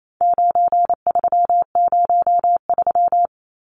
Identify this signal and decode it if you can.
Morse code, heard as a keyed tone, about 28 wpm, 700 Hz. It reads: 9303
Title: Numbers - 4-Digit 28wpm